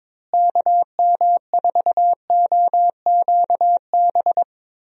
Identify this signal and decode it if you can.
Morse code, heard as a keyed tone, about 22 wpm, 700 Hz. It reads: KM4OQB
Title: USA Call Signs - 6 Characters - Repeated 2x - 22wpm